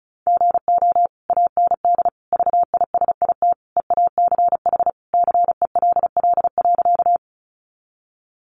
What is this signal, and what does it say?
Morse code, heard as a keyed tone, about 34 wpm, 700 Hz. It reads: GO AND VISIT EACH CELL.
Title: Sentences from Top 1100 Words 35wpm